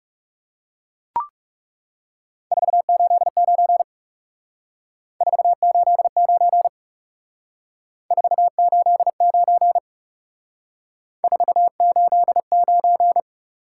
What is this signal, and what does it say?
Morse code, heard as a keyed tone, about 36 wpm, 700 Hz. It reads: 489 489 489 489
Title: Speed-Racing Top 180 QSO Elements 30wpm